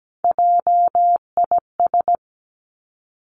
Morse code, heard as a keyed tone, about 17 wpm, 700 Hz. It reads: JIS
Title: Three Challenging Letter-Number Combinations 17wpm